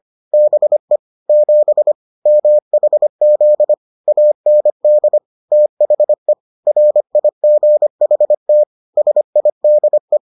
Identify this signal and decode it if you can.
Morse code, heard as a keyed tone, about 25 wpm, 600 Hz. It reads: BE 7 MHZ AND THE RIGHT SIDE